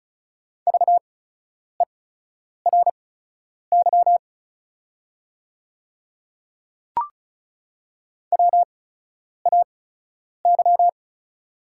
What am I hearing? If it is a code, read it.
Morse code, heard as a keyed tone, about 36 wpm, spaced as for 10 wpm, 700 Hz. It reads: VERY WAY
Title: 8x Character Spacing - Top 200 Words - 35wpm